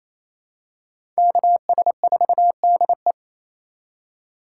Morse code, heard as a keyed tone, about 28 wpm, 700 Hz. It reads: KS4DE
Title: USA Call Signs - 5 Characters - 2x Character Spacing - 28wpm